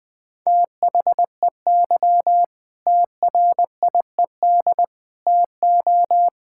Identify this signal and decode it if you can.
Morse code, heard as a keyed tone, about 20 wpm, 700 Hz. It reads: THEY TRIED TO
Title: Sentences with 8-Word Phrases - 20wpm